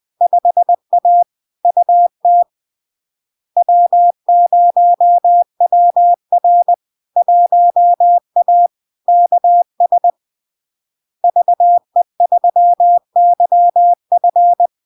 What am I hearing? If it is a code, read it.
Morse code, heard as a keyed tone, about 20 wpm, 700 Hz. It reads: 5A UT W0WR 1A KS VE3YF
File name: LARK-FD-2022-Log-in-CW-20wpm-700-Hz